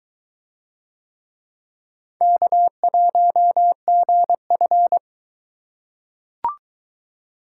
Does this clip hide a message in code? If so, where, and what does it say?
Yes: Morse code from 2.2 to 5.0 s, heard as a keyed tone, about 20 wpm, 700 Hz. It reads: K1GF